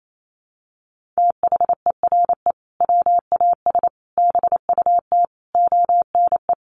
Morse code, heard as a keyed tone, about 28 wpm, 700 Hz. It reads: THERE WAS BUT ONE
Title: Sentences from Top 1400 Words 28wpm